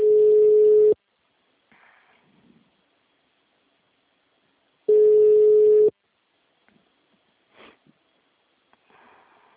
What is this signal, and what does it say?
Ringback tone (Europe, 425 Hz).
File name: out- -2002- - - .0